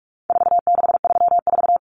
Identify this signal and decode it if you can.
Morse code, heard as a keyed tone, about 44 wpm, 700 Hz. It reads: 4634
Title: Numbers - 4-Digit 45wpm